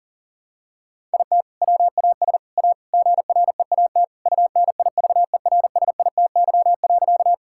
Morse code, heard as a keyed tone, about 40 wpm, 700 Hz. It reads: IT WAS A GREAT UNIVERSITY.
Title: Sentences from Top 1000 Words - Review Part 5 - 40wpm